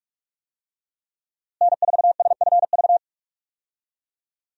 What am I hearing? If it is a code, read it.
Morse code, heard as a keyed tone, about 41 wpm, 700 Hz. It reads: N4SFV
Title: USA Call Signs - 5 Characters - Repeated 2x - 45wpm